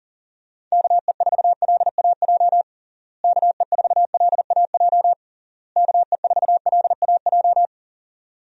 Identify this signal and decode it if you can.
Morse code, heard as a keyed tone, about 40 wpm, 700 Hz. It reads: KE4LAJ KE4LAJ KE4LAJ
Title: USA Call Signs - 6 Characters - Repeated 3x - 40wpm